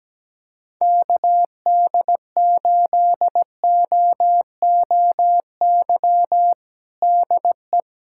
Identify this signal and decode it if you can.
Morse code, heard as a keyed tone, about 17 wpm, 700 Hz. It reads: KD8OOY DE
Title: USA Callsign DE Callsign - 6 Characters - 17wpm